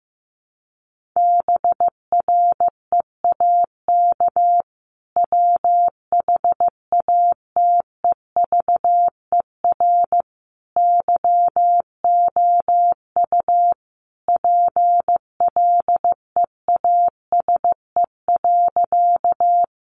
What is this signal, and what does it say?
Morse code, heard as a keyed tone, about 15 wpm, 700 Hz. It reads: BREAK WHATEVER YOU PLEASE.
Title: Sentences from Top 1500 Words 15wpm